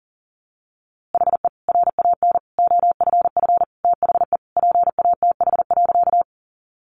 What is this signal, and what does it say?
Morse code, heard as a keyed tone, about 40 wpm, 700 Hz. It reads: HE RAN OFF THE PATH.
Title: A Sound Of Thunder 40wpm